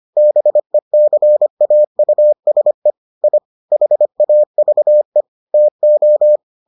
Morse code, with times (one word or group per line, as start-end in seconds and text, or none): 0.2-2.9 BECAUSE
3.2-3.4 I
3.7-5.2 HAVE
5.5-6.4 TO